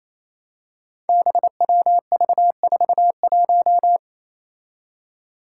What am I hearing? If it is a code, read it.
Morse code, heard as a keyed tone, about 29 wpm, 700 Hz. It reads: BWV41
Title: Five Challenging Letter-Number Combinations 28wpm